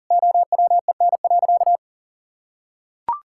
Morse code, heard as a keyed tone, about 37 wpm, 700 Hz. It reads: OWEN.
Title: Pangram Challenge v2 - 40wpm